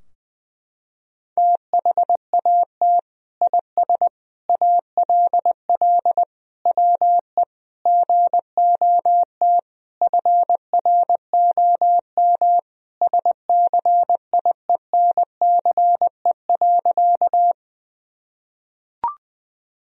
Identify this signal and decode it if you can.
Morse code, heard as a keyed tone, about 20 wpm, 700 Hz. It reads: THAT IS ALL WE GOT FROM SCIENCE.
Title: Sentences from Top 1000 Words - Review Part 2 - 20wpm